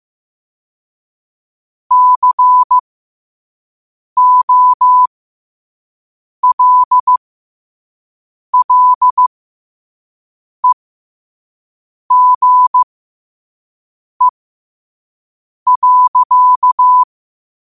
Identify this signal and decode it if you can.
Morse code, heard as a keyed tone, about 15 wpm, spaced as for 5 wpm, 1000 Hz. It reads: COLLEGE.
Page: Quotes for Sat, 16 Aug 2025 in Morse Code at 5 words per minute.